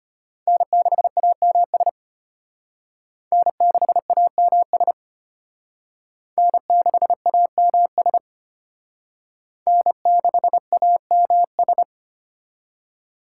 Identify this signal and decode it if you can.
Morse code, heard as a keyed tone, about 31 wpm, 700 Hz. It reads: N6AMS N6AMS N6AMS N6AMS
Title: Speed Racing - USA 1x3 Call Signs 25wpm